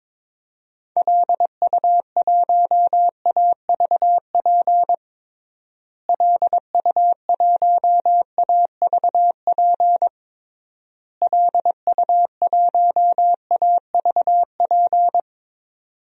Morse code, heard as a keyed tone, about 22 wpm, 700 Hz. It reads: LU1AVP LU1AVP LU1AVP
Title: DXCC-100 Call Signs - 6 Characters - Repeated 3x - 22wpm